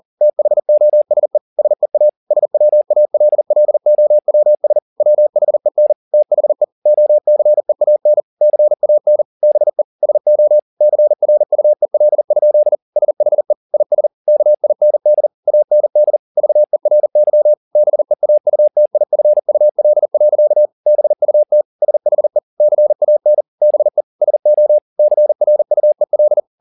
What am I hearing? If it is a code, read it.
Morse code, heard as a keyed tone, about 40 wpm, 600 Hz. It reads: THOSE SEA SWALLOWS WHEN THE OCEAN CAN BE SO CRUEL? SHE IS KIND AND VERY BEAUTIFUL. BUT SHE CAN BE SO CRUEL